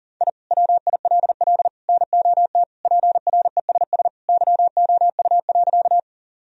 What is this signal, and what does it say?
Morse code, heard as a keyed tone, about 40 wpm, 700 Hz. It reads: I WILL NOT PRESS YOU.
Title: Sentences from Top 1000 Words - Review Part 1 - 40wpm